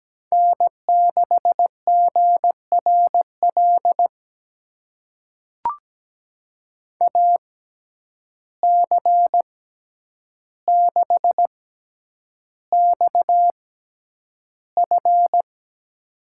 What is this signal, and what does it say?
Morse code, heard as a keyed tone, about 17 wpm, spaced as for 9 wpm, 700 Hz. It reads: N6GRL AC6XF
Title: USA Call Signs - 5 Characters - 6x Character Spacing - 17wpm